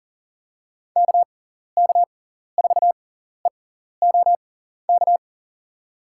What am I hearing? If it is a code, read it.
Morse code, heard as a keyed tone, about 40 wpm, spaced as for 14 wpm, 700 Hz. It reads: KK4EOK
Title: USA Call Signs - 6x Character Spacing - 40wpm